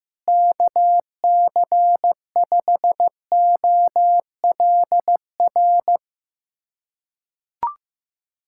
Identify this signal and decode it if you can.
Morse code, heard as a keyed tone, about 15 wpm, 700 Hz. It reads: KC5OLR
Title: USA Callsign DE Callsign - 6 Characters - 15wpm